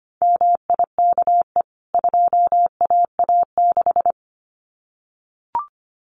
Morse code, heard as a keyed tone, about 25 wpm, 700 Hz. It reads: MIKE 2AA6
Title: CWT Exchange 25wpm